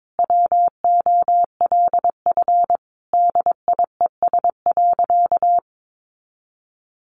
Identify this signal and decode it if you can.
Morse code, heard as a keyed tone, about 22 wpm, 700 Hz. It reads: WOLF DIES.
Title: Additional Easy Phrases 22wpm